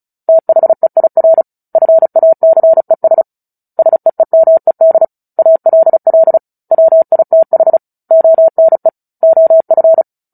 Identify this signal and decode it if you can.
Morse code, heard as a keyed tone, about 34 wpm, 650 Hz. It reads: THEIR FACES SEEMED ALL WITH ONE OF
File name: Book Two / 09-Chapter 9 - 35 WPM